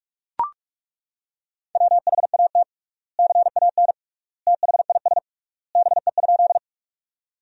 Morse code, heard as a keyed tone, about 42 wpm, 700 Hz. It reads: WHAT CAN THIS BE?
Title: 3x Word-Spacing - Sentences from Top 100 Words - 45wpm